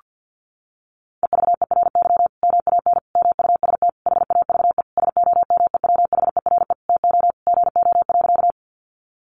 Morse code, heard as a keyed tone, about 48 wpm, 700 Hz. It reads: EVERY MAN MUST HAVE SOMEWHERE TO GO.
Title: Sentences from Top 1400 Words 50wpm